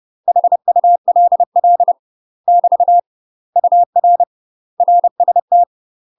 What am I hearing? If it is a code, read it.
Morse code, heard as a keyed tone, about 30 wpm, 700 Hz. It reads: HULL <BT> UR RST
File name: UK MASTER QSOs TRAINING / 30WPM / UK MASTER QSOs 0001